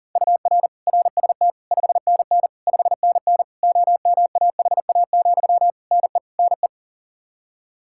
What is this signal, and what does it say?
Morse code, heard as a keyed tone, about 40 wpm, 700 Hz. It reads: UR RST 5NN 5NN OMAHA, NE NE
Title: Practice QSO - Exchange 1 - Beginning - 40wpm